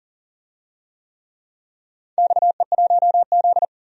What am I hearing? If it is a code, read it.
Morse code, heard as a keyed tone, about 38 wpm, 700 Hz. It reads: XE1Z
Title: DXCC-100 Call Signs - 4 Characters - 40wpm